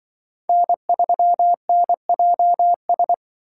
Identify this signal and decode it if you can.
Morse code, heard as a keyed tone, about 24 wpm, 700 Hz. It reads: N3NJS
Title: Speed Racing - USA 1x3 Call Signs 20wpm